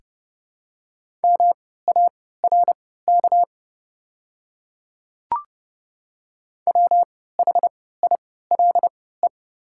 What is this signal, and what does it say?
Morse code, heard as a keyed tone, about 30 wpm, spaced as for 17 wpm, 690 Hz. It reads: MARK WHILE
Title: 3x Character Spacing - Top 300 Words - 30wpm